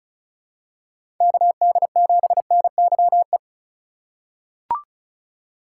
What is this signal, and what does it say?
Morse code, heard as a keyed tone, about 36 wpm, 700 Hz. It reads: KD7NYE
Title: USA Call Signs - 3x Character Spacing - 35wpm